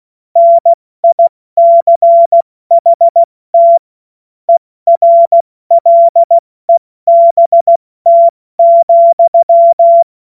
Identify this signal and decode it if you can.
Morse code, heard as a keyed tone, about 16 wpm, 680 Hz. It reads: NICHT ERLEBT,
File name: Morsekurs / Witze / Witz 02 April 2024 70 F80